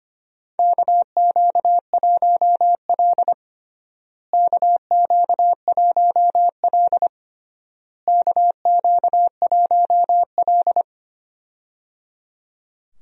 Morse code, heard as a keyed tone, about 25 wpm, 700 Hz. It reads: KQ1L KQ1L KQ1L
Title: USA Call Signs - 4 Characters - Repeated 3x - 25wpm